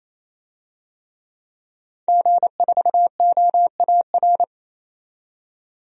Morse code, heard as a keyed tone, about 28 wpm, 700 Hz. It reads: G4OAR
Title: DXCC-100 Call Signs - 5 Characters - 28wpm